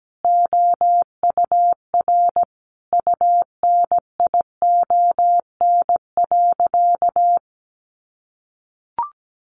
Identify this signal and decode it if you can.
Morse code, heard as a keyed tone, about 17 wpm, 700 Hz. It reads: OUR UNION.
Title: Sentences from Top 1400 Words 17wpm